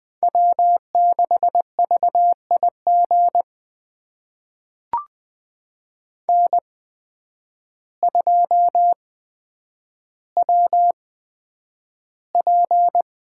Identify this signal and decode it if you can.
Morse code, heard as a keyed tone, about 20 wpm, spaced as for 9 wpm, 700 Hz. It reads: W6VIG N2WP
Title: USA Call Signs - 8x Character Spacing - 20wpm